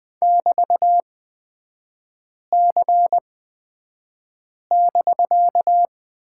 Morse code, heard as a keyed tone, about 20 wpm, spaced as for 5 wpm, 700 Hz. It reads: <BT>C<BK>
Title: 250 Lesson 40e - BK - Three Characters 20wpm